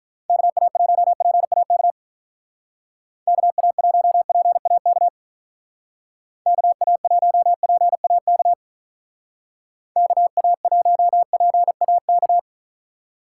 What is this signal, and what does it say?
Morse code, heard as a keyed tone, about 43 wpm, 700 Hz. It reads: KA1PAK KA1PAK KA1PAK KA1PAK
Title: Speed Racing - USA 2x3 Call Signs 35wpm